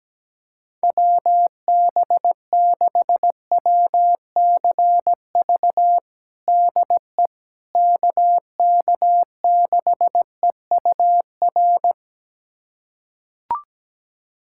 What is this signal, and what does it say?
Morse code, heard as a keyed tone, about 17 wpm, 700 Hz. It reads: WB6WCV DE KK6EUR
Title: USA Callsign DE Callsign - 6 Characters - 17wpm